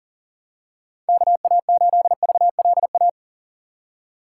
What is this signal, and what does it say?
Morse code, heard as a keyed tone, about 40 wpm, 700 Hz. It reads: KA8VLA
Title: USA Call Signs - 6 Characters - 2x Character Spacing - 40wpm